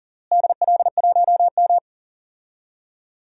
Morse code, heard as a keyed tone, about 37 wpm, 700 Hz. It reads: DL1M
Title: DXCC-100 Call Signs - 4 Characters - Repeated 2x - 40wpm